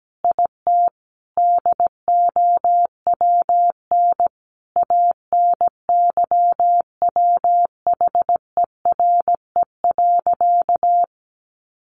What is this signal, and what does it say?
Morse code, heard as a keyed tone, about 17 wpm, 700 Hz. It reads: IT DOWN ANYWHERE.
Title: Sentences from Top 1400 Words 17wpm